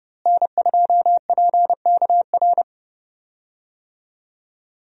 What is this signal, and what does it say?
Morse code, heard as a keyed tone, about 30 wpm, 700 Hz. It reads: N2PKR
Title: USA Call Signs - 5 Characters - Repeated 3x - 30wpm